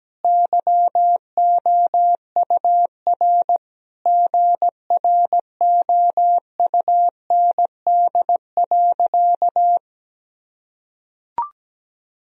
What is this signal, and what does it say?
Morse code, heard as a keyed tone, about 17 wpm, 700 Hz. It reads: YOUR GROUND.
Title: Sentences from Top 1000 Words - Review Part 2 - 17wpm